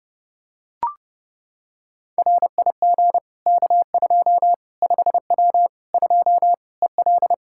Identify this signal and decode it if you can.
Morse code, heard as a keyed tone, about 30 wpm, 700 Hz. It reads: RIG K2 5W 2 EL
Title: Practice QSO - Exchange 2 - Info - 30wpm